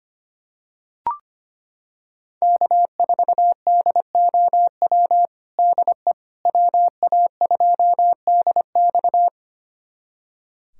Morse code, heard as a keyed tone, about 25 wpm, 700 Hz. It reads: K4DOW DE WA2DX
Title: USA Callsign DE Callsign - 5 Characters - 25wpm